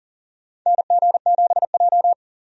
Morse code, heard as a keyed tone, about 40 wpm, 700 Hz. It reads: NG7J
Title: USA Call Signs - 4 Characters - 40wpm